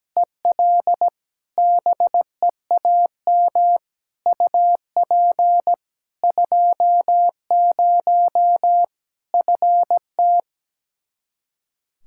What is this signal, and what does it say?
Morse code, heard as a keyed tone, about 17 wpm, 700 Hz. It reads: EL BEAM UP 20 FT
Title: Practice QSO - Exchange 2 - Info - 17wpm